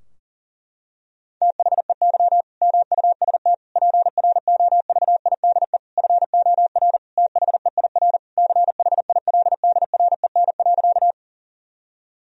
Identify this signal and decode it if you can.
Morse code, heard as a keyed tone, about 40 wpm, 700 Hz. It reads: THEY MUST PROVIDE FOR THEIR CHILDREN.
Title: Sentences from Top 1000 Words - Review Part 3 - 40wpm